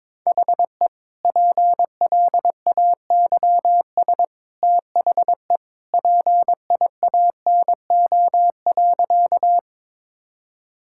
Morse code, heard as a keyed tone, about 22 wpm, 700 Hz. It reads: HE PLAYS THE PIANO.
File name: practice / 22wpm / Easy Sentences with Hints - 22wpm